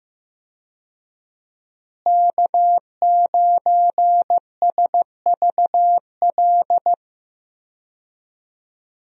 Morse code, heard as a keyed tone, about 15 wpm, 700 Hz. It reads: K9SVL
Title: USA Call Signs - 5 Characters - Repeated 3x - 15wpm